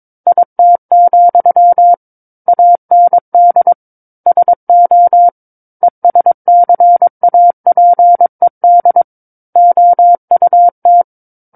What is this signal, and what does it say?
Morse code, heard as a keyed tone, about 22 wpm, 700 Hz. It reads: IT, AND SO ESCAPED OUT
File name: Book One / 13-Chapter 13 - 22 WPM